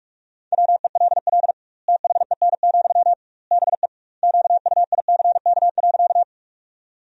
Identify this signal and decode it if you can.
Morse code, heard as a keyed tone, about 46 wpm, 700 Hz. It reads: WELL THEN, BE QUICK.
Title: 2x Word-Spacing - Sentences from Top 500 Words - 45wpm